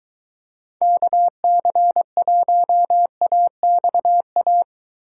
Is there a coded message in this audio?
Yes: Morse code from 0.8 to 4.6 s, heard as a keyed tone, about 21 wpm, 700 Hz. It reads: KC1AXA